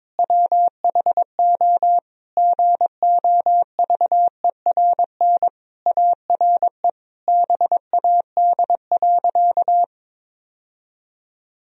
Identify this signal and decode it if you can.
Morse code, heard as a keyed tone, about 22 wpm, 700 Hz. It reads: WHO GOVERN ARE BAD.
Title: Sentences from Top 1000 Words - Review Part 2 - 22wpm